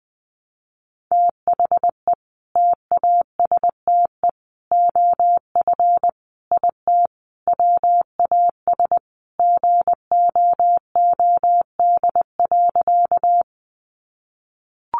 Morse code, heard as a keyed tone, about 20 wpm, 700 Hz. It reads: THE TASTE OF IT WAS GOOD.
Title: Sentences from Top 1100 Words 20wpm